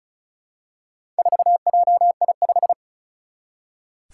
Morse code, heard as a keyed tone, about 36 wpm, 700 Hz. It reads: 4JI5
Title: Four Challenging Letter-Number Combinations 35wpm